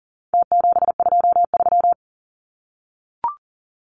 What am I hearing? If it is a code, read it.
Morse code, heard as a keyed tone, about 40 wpm, 700 Hz. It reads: T723
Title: Numbers - Contest Style - 4-Digit 40wpm